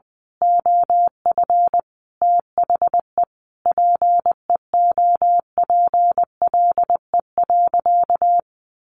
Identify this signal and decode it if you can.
Morse code, heard as a keyed tone, about 19 wpm, 700 Hz. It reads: OF THE PEOPLE.
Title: Sentences from Top 1500 Words 20wpm